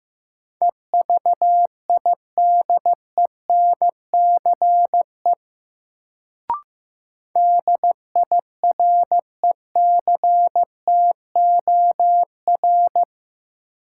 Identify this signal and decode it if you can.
Morse code, heard as a keyed tone, about 15 wpm, 700 Hz. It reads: EVIDENCE DIRECTOR
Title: Sentences from Top 1300 Words - New Word Drill Alt - 15wpm